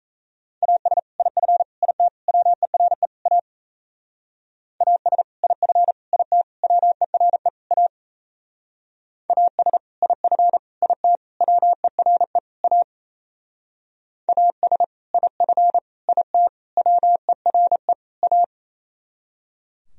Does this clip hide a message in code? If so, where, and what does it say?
Yes: Morse code from 0.6 to 18.4 s, heard as a keyed tone, about 35 wpm, 700 Hz. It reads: AS IF IT WERE A AS IF IT WERE A AS IF IT WERE A AS IF IT WERE A